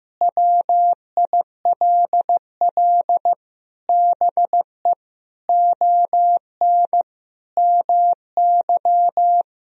Morse code, heard as a keyed tone, about 16 wpm, 700 Hz. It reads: WILL BE ON MY
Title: Sentences from Top 1000 Words - Review Part 1 - 15wpm